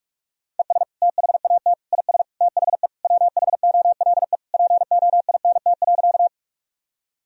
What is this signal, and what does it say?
Morse code, heard as a keyed tone, about 42 wpm, 700 Hz. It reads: ES THAT IS THE WHOLE POINT.
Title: Sentences from Top 400 Words and Abbreviations 45wpm